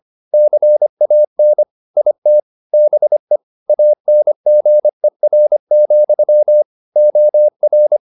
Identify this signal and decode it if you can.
Morse code, heard as a keyed tone, about 25 wpm, 600 Hz. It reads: CAN IT BE ANGER, OR